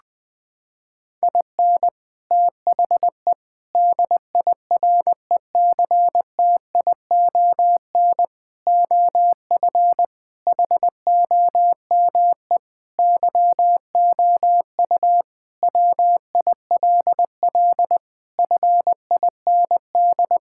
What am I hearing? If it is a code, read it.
Morse code, heard as a keyed tone, about 21 wpm, 700 Hz. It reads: IN THE DIRECTION OF HOME YOU WILL FIND
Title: Sentences with 4-Word Phrases - 20wpm